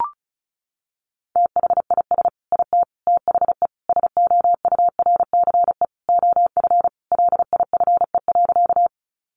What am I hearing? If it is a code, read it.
Morse code, heard as a keyed tone, about 34 wpm, 700 Hz. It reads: THIS IT THE SOURCE OF LIFE.
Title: Sentences from Top 800 Words 35wpm